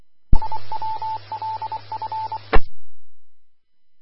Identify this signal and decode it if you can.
Morse code, heard as a keyed tone, about 24 wpm, 880 Hz. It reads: IWLF
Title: Austin, TX Amateur Repeaters